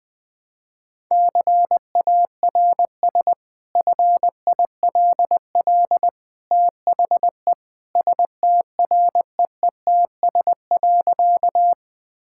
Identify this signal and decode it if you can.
Morse code, heard as a keyed tone, about 20 wpm, 700 Hz. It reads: CARS FILL THE STREETS.
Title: Easy Sentences with Hints - 20wpm